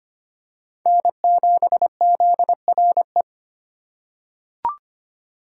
Morse code, heard as a keyed tone, about 25 wpm, 700 Hz. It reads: N7ZRE
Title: USA Call Signs - 5 Characters - 3x Character Spacing - 25wpm